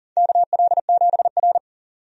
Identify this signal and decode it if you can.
Morse code, heard as a keyed tone, about 40 wpm, 700 Hz. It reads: KL7R
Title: USA Call Signs - 4 Characters - 40wpm